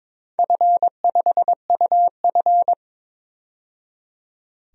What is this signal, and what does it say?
Morse code, heard as a keyed tone, about 22 wpm, 700 Hz. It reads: F5UF
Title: DXCC-100 Call Signs - 4 Characters - 22wpm